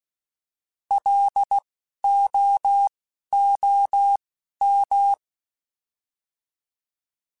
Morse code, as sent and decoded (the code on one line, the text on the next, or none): .-.. / --- / --- / --
L O O M